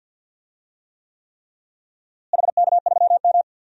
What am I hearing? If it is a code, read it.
Morse code, heard as a keyed tone, about 52 wpm, 700 Hz. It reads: HK3M
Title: DXCC-100 Call Signs - 4 Characters - Repeated 3x - 50wpm